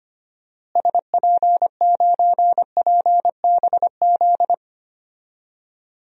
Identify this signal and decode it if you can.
Morse code, heard as a keyed tone, about 25 wpm, 700 Hz. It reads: SP9PBZ
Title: DXCC-100 Call Signs - 6 Characters - Repeated 2x - 25wpm